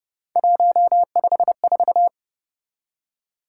Morse code, heard as a keyed tone, about 29 wpm, 700 Hz. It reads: 154